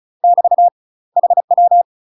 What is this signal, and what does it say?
Morse code, heard as a keyed tone, about 36 wpm, 700 Hz. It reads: <BT> HW
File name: UK MASTER QSOs TRAINING / 35WPM / UK MASTER QSOs 0011